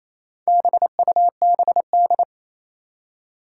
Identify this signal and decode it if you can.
Morse code, heard as a keyed tone, about 27 wpm, 700 Hz. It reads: BUBD
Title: Four Challenging Letter-Number Combinations 28wpm